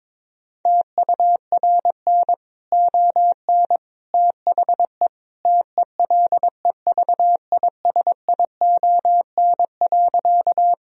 Morse code, heard as a keyed tone, about 22 wpm, 700 Hz. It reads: TURN ON THE TELEVISION.
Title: Sentences from Top 1000 Words - Review Part 4 - 22wpm